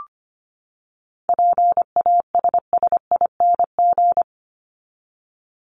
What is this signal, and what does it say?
Morse code, heard as a keyed tone, about 25 wpm, 700 Hz. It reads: PASSING
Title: ING Suffix - Top 100 Words with Base Words 25wpm